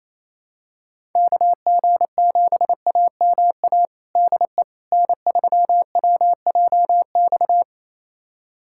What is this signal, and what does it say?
Morse code, heard as a keyed tone, about 28 wpm, 700 Hz. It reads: KG7AMA DE N3WJX
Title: USA Callsign DE Callsign 28wpm